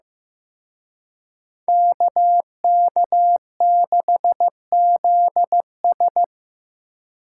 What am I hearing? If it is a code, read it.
Morse code, heard as a keyed tone, about 18 wpm, 700 Hz. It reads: KK6ZS
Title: Speed Racing - USA 2x2 Call Signs 15wpm